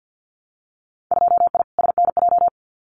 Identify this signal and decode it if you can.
Morse code, heard as a keyed tone, about 50 wpm, 700 Hz. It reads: 2I SNJ
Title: Winter Field Day Exchanges Encore 50wpm